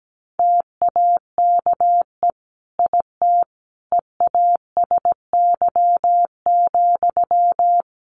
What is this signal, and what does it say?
Morse code, heard as a keyed tone, about 17 wpm, 700 Hz. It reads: TAKE IT EASY,
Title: A Sound Of Thunder 17wpm